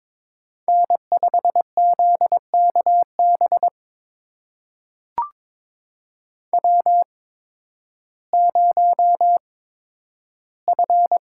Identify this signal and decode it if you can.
Morse code, heard as a keyed tone, about 21 wpm, spaced as for 12 wpm, 700 Hz. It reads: N5ZKB W0F
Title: USA Call Signs - 5 Characters - 8x Character Spacing - 22wpm